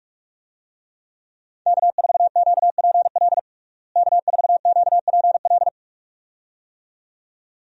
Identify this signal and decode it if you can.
Morse code, heard as a keyed tone, about 45 wpm, 700 Hz. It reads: K4QPL K4QPL
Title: USA Call Signs - 5 Characters - Repeated 2x - 45wpm